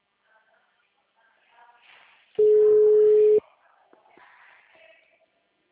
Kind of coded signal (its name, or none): telephone call progress tone